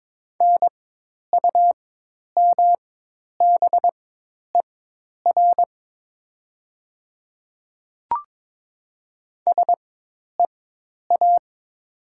Morse code, heard as a keyed tone, about 22 wpm, spaced as for 10 wpm, 700 Hz. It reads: NUMBER SEA
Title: 4x Character Spacing - Top 300 Words - 22wpm